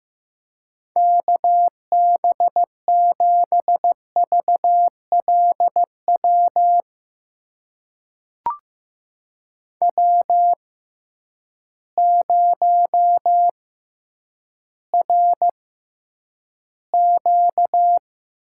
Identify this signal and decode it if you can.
Morse code, heard as a keyed tone, about 15 wpm, spaced as for 9 wpm, 700 Hz. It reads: KB7VLW W0RQ
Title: USA Call Signs - 6x Character Spacing - 15wpm